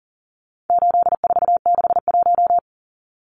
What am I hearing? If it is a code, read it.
Morse code, heard as a keyed tone, about 40 wpm, 700 Hz. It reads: 8461
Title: Numbers - 4-Digit 40wpm